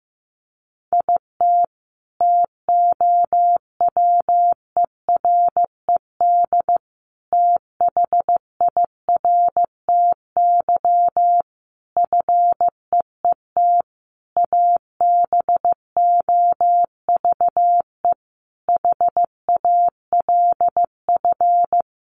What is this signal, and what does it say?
Morse code, heard as a keyed tone, about 15 wpm, 700 Hz. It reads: IT TOWERED THIRTY FEET ABOVE HALF
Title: A Sound Of Thunder 15wpm